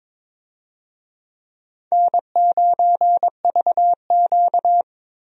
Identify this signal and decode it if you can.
Morse code, heard as a keyed tone, about 22 wpm, 700 Hz. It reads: N9VQ
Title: USA Call Signs - 4 Characters - Repeated 2x - 22wpm